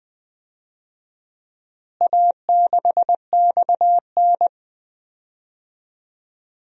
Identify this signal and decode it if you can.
Morse code, heard as a keyed tone, about 21 wpm, 700 Hz. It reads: A6XN
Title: DXCC-100 Call Signs - 4 Characters - Repeated 3x - 20wpm